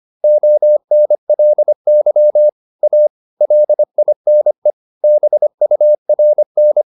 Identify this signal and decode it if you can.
Morse code, heard as a keyed tone, about 25 wpm, 600 Hz. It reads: ONLY A LINE BURN
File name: the old man and the sea part 1 25wpm